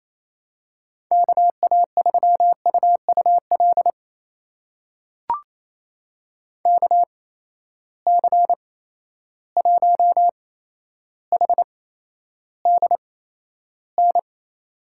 Morse code, heard as a keyed tone, about 28 wpm, spaced as for 12 wpm, 700 Hz. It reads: KA3UUL KC1HDN
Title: USA Call Signs - 8x Character Spacing - 28wpm